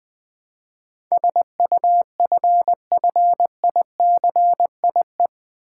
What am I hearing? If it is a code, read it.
Morse code, heard as a keyed tone, about 20 wpm, 700 Hz. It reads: SUFFICIE
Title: Build A Word - 4 Letters At A Time - 10-Letter Words - 20wpm